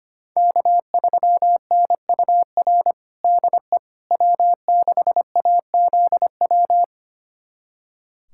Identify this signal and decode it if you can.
Morse code, heard as a keyed tone, about 25 wpm, 700 Hz. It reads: K3NUR DE W6AZW
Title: USA Callsign DE Callsign - 5 Characters - 25wpm